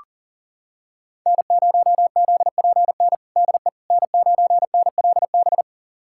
Morse code, heard as a keyed tone, about 40 wpm, 700 Hz. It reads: N0ZPN DE N9NLB
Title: USA Callsign DE Callsign 40wpm